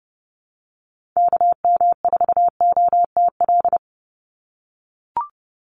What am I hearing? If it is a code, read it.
Morse code, heard as a keyed tone, about 30 wpm, 700 Hz. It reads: KM4OTL
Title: USA Call Signs 30wpm